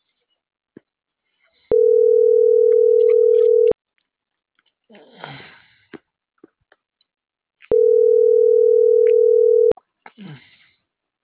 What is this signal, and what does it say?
Ringback tone (North America), 440 + 480 Hz.